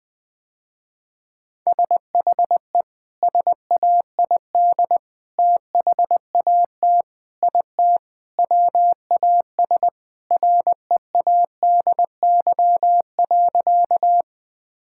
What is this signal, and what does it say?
Morse code, heard as a keyed tone, about 21 wpm, 700 Hz. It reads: SHE SAID THAT IT WAS READY.
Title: Sentences with 3-Word Phrases - Spoken Beginning - 20wpm